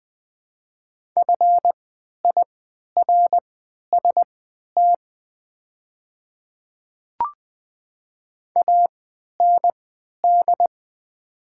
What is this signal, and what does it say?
Morse code, heard as a keyed tone, about 20 wpm, spaced as for 11 wpm, 700 Hz. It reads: FIRST AND
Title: 3x Character Spacing - Top 100 Words - 20wpm